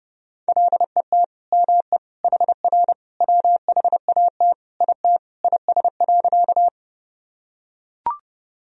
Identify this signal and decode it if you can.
Morse code, heard as a keyed tone, about 30 wpm, 700 Hz. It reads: LET ME HR WHAT IT IS.
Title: Sentences from Top 300 Words and Abbreviations 30wpm